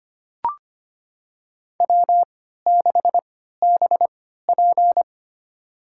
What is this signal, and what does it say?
Morse code, heard as a keyed tone, about 25 wpm, spaced as for 14 wpm, 700 Hz. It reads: W6BP
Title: USA Call Signs - 4 Characters - 3x Character Spacing - 25wpm